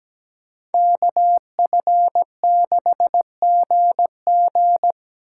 Morse code, heard as a keyed tone, about 20 wpm, 700 Hz. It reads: KF6GG
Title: Speed Racing - USA Call Signs 17wpm